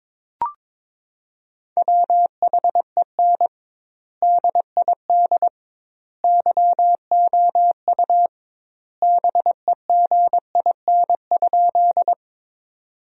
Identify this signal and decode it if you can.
Morse code, heard as a keyed tone, about 21 wpm, 700 Hz. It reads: WHEN DID YOU BEGIN?
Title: 2x Word-Spacing - Sentences from Top 300 Words - 22wpm